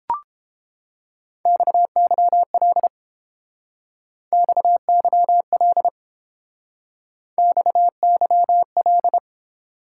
Morse code, heard as a keyed tone, about 29 wpm, 700 Hz. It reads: XYL XYL XYL